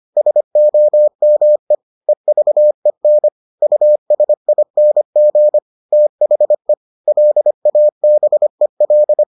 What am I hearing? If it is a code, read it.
Morse code, heard as a keyed tone, about 25 wpm, 600 Hz. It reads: SOME EVEN USING THE LABEL